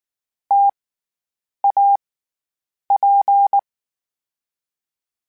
Morse code, heard as a keyed tone, about 19 wpm, 800 Hz. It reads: T A P
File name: Variable Spacing / Variable Spacing-mp3s / T AOEs @18